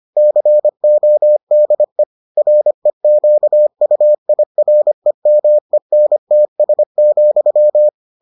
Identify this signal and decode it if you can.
Morse code, heard as a keyed tone, about 25 wpm, 600 Hz. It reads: CODE REQUIREMENTS,